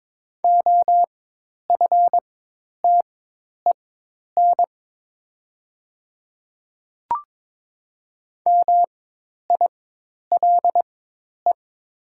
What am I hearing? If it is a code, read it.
Morse code, heard as a keyed tone, about 23 wpm, spaced as for 10 wpm, 700 Hz. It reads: OFTEN MILE
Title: 4x Character Spacing - Top 300 Words - 22wpm